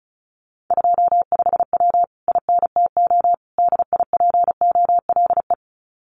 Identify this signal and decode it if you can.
Morse code, heard as a keyed tone, about 36 wpm, 700 Hz. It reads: 25W INTO DIPOLE
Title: Common QSO Phrases Encore 35wpm